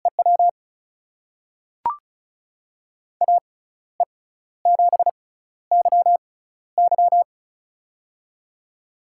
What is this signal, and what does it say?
Morse code, heard as a keyed tone, about 34 wpm, spaced as for 14 wpm, 700 Hz. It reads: EW AE7YY
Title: USA Call Signs - 6x Character Spacing - 35wpm